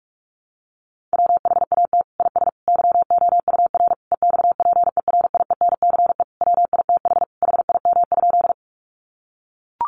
Morse code, heard as a keyed tone, about 44 wpm, 700 Hz. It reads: WHAT IS YOUR EXPERIENCE WITH HIM?
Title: Sentences from Top 600 Words 45wpm